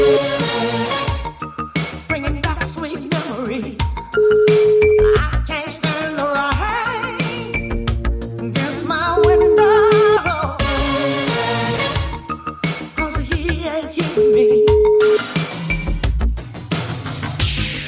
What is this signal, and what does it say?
Ringback tone (Europe, 425 Hz), with noise.